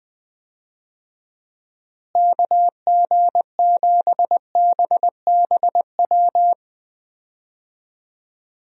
Morse code, heard as a keyed tone, about 21 wpm, 700 Hz. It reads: KG7BBW
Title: USA Call Signs - 6 Characters - Repeated 3x - 20wpm